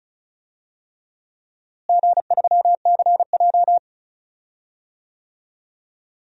Morse code, heard as a keyed tone, about 36 wpm, 700 Hz. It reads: G3CJ
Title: DXCC-100 Call Signs - 4 Characters - Repeated 3x - 35wpm